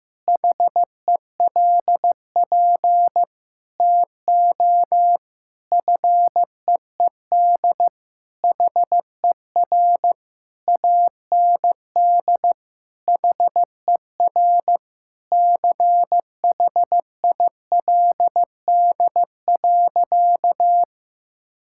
Morse code, heard as a keyed tone, about 16 wpm, 700 Hz. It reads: HELP TO FEED HER AND HER CHILD.
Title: Sentences from Top 1600 Words 15wpm